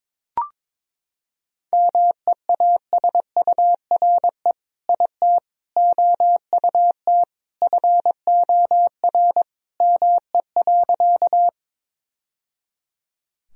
Morse code, heard as a keyed tone, about 22 wpm, 700 Hz. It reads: MEASURE IT OUT FOR ME.
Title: Sentences from Top 1000 Words - Review Part 2 - 22wpm